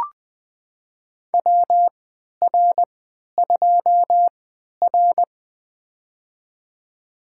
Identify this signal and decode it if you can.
Morse code, heard as a keyed tone, about 20 wpm, spaced as for 11 wpm, 700 Hz. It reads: WR2R
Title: USA Call Signs - 4 Characters - 3x Character Spacing - 20wpm